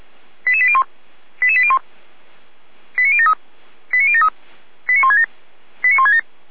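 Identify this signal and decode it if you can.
Five-tone selective call (ZVEI), heard as quick runs of five tones, tones of about 70 ms. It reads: 88801 88801 89073 89073 89167 89167
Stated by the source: ZVEI five-tone call 89163 received by SDR.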